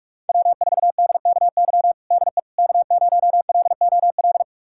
Morse code, heard as a keyed tone, about 46 wpm, 700 Hz. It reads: W4DKY DE K0LOL
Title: USA Callsign DE Callsign - 5 Characters - 45wpm